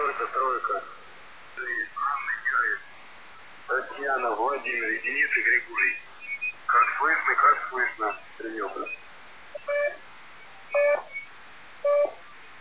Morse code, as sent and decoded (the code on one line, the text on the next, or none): none